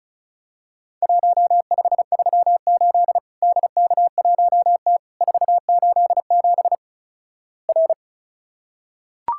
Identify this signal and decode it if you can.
Morse code, heard as a keyed tone, about 36 wpm, 700 Hz. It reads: 1538 DK1T 487 R
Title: Worked All Europe DX Contest - QTC Exchange - 35wpm